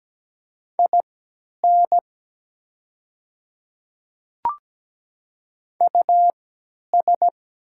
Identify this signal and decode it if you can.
Morse code, heard as a keyed tone, about 17 wpm, spaced as for 10 wpm, 690 Hz. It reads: IN US
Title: 3x Character Spacing - Top 300 Words - 17wpm